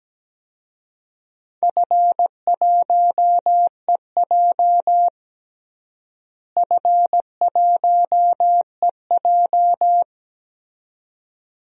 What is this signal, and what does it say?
Morse code, heard as a keyed tone, about 17 wpm, 700 Hz. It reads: F1EJ F1EJ
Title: DXCC-100 Call Signs - 4 Characters - Repeated 2x - 17wpm